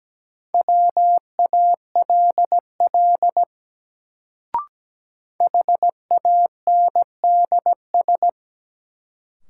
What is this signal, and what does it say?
Morse code, heard as a keyed tone, about 17 wpm, 700 Hz. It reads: WALL HANDS
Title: Sentences from Top 1000 Words - Review Part 6 - New Word Drill - 17wpm